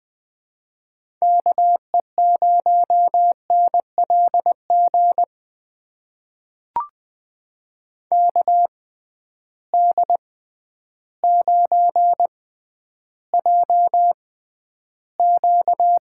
Morse code, heard as a keyed tone, about 20 wpm, spaced as for 11 wpm, 700 Hz. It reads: KE0NLG KD9JQ
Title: USA Call Signs - 6 Characters - 6x Character Spacing - 20wpm